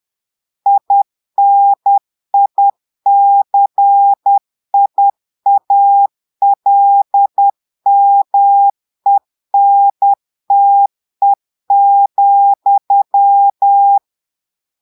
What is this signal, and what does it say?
Morse code, heard as a keyed tone, about 10 wpm, 800 Hz. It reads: INICIALMENTE,
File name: portugais texte en clair 2 vitesse 10 mots minutes